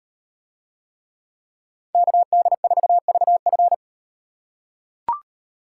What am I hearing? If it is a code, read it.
Morse code, heard as a keyed tone, about 38 wpm, 700 Hz. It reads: KD4VF